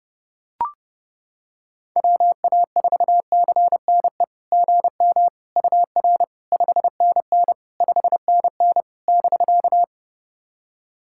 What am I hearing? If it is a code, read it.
Morse code, heard as a keyed tone, about 30 wpm, 700 Hz. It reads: WA4CNE GM UR 5NN 5NN <BK>
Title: SOTA QSO Practice 30wpm